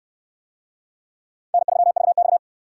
Morse code, heard as a keyed tone, about 68 wpm, 700 Hz. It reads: N3VX